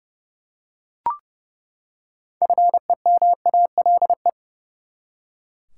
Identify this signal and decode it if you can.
Morse code, heard as a keyed tone, about 30 wpm, 700 Hz. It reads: FEMALE
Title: Six Letter Words - 30wpm